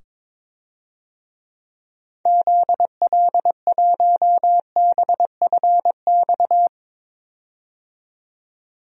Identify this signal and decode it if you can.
Morse code, heard as a keyed tone, about 22 wpm, 700 Hz. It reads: ZL1BFX
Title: DXCC-100 Call Signs - 6 Characters - 22wpm